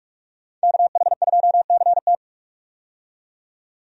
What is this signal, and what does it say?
Morse code, heard as a keyed tone, about 43 wpm, 700 Hz. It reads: KH2CT
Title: USA Call Signs - 5 Characters - 45wpm